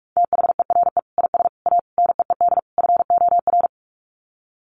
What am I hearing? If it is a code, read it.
Morse code, heard as a keyed tone, about 44 wpm, 700 Hz. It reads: THERE IS A NEED FOR
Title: Sets of 5 Words Encore2 45wpm